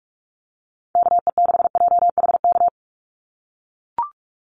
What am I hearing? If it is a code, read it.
Morse code, heard as a keyed tone, about 44 wpm, 700 Hz. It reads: KE6JHK
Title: USA Call Signs 45wpm